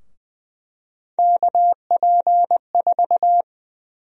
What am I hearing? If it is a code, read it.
Morse code, heard as a keyed tone, about 20 wpm, 700 Hz. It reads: KP4
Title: USA Call Sign Prefixes - 20wpm